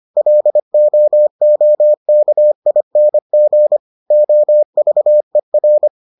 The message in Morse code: LOOKING OVER

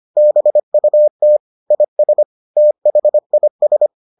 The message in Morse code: BUT IS THIS

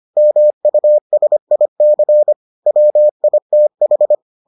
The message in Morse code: MUSIC WITH

